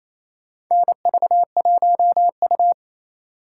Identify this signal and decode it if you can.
Morse code, heard as a keyed tone, about 28 wpm, 700 Hz. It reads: NV1U
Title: K1USN SST QSO Practice - 28wpm